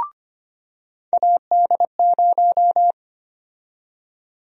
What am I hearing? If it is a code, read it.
Morse code, heard as a keyed tone, about 25 wpm, 700 Hz. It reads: AD0
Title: USA Call Sign Prefixes - 25wpm